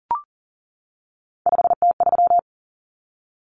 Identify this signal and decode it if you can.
Morse code, heard as a keyed tone, about 40 wpm, 700 Hz. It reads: 5T3
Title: Common Cut Numbers - 3-Digit 40wpm